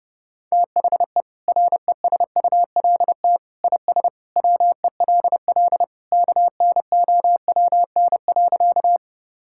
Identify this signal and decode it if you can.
Morse code, heard as a keyed tone, about 30 wpm, 700 Hz. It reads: THE RESULT IS WELL KNOWN.
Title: Sentences from Top 1000 Words - Review Part 5 - 30wpm